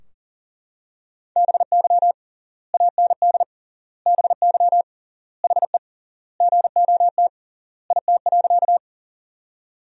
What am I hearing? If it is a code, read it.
Morse code, heard as a keyed tone, about 40 wpm, 700 Hz. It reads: BY AND BY HE GOT IT.
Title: 3x Word-Spacing - Sentences from Top 300 Words - 40wpm